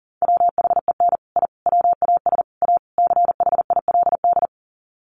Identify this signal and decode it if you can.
Morse code, heard as a keyed tone, about 40 wpm, 700 Hz. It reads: WHEN I WAS A CHILD
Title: Sets of 5 Words Encore 40wpm